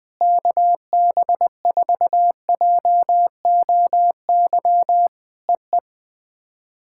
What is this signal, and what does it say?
Morse code, heard as a keyed tone, about 20 wpm, 700 Hz. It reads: KB4JOY EE
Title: POTA QSO Practice - 20wpm